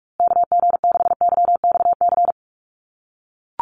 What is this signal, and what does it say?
Morse code, heard as a keyed tone, about 44 wpm, 700 Hz. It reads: KG6YXC
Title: USA Call Signs Repeated 2x 45wpm